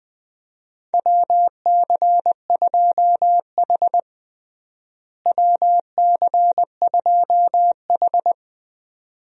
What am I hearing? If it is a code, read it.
Morse code, heard as a keyed tone, about 20 wpm, 700 Hz. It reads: WC2H WC2H
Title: USA Call Signs - 4 Characters - Repeated 2x - 20wpm